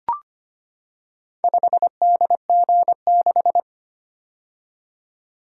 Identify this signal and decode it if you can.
Morse code, heard as a keyed tone, about 25 wpm, 700 Hz. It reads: 5DG6
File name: practice / 25wpm / Four Letter-Number Combinations - 25wpm